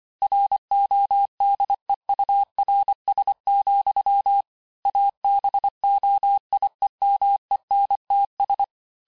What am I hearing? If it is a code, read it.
Morse code, heard as a keyed tone, about 25 wpm, 800 Hz. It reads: RODEURS, ABOIEMENTS
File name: cours-868-teste-clair-vitesse-23mpm